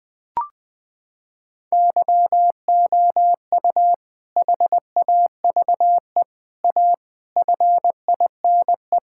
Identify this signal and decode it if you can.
Morse code, heard as a keyed tone, about 21 wpm, 700 Hz. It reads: YOU HAVE A FINE
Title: Sentences from Top 1000 Words - Review Part 6 - 20wpm